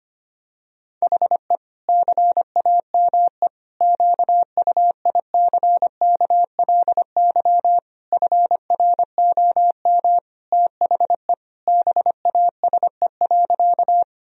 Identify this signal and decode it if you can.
Morse code, heard as a keyed tone, about 25 wpm, 700 Hz. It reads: HE CAME QUICKLY FROM THE BASE.
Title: Sentences from Top 1000 Words - Review Part 1 - 25wpm